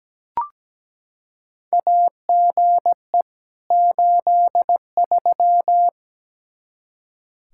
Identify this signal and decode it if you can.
Morse code, heard as a keyed tone, about 17 wpm, 700 Hz. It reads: AGE 83
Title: Practice QSO - Exchange 3 - More Info - 17wpm